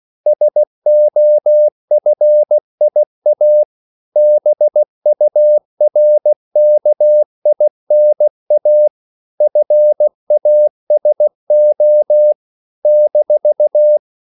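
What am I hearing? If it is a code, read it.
Morse code, heard as a keyed tone, about 16 wpm, 600 Hz. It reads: SOFIA BURKINA FASO -
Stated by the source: CW practice of national capitals at 16 words per minute, from ON6ZQ.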